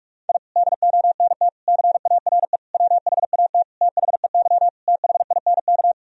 Morse code, heard as a keyed tone, about 46 wpm, 700 Hz. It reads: I DONT CARE WHAT THEY THINK
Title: Sentences from Top 1000 Words - Review Part 1 - 45wpm